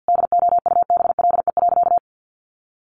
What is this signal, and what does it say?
Morse code, heard as a keyed tone, about 50 wpm, 700 Hz. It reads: DOUBLE.
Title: Sentences from Top 1400 Words 50wpm